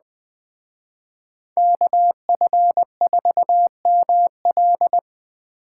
Morse code, heard as a keyed tone, about 20 wpm, 700 Hz. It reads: KF4ML